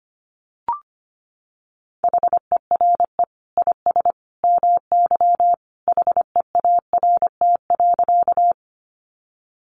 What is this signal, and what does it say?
Morse code, heard as a keyed tone, about 25 wpm, 700 Hz. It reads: HERE IS MY HEART.
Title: Sentences from Top 1200 Words 25wpm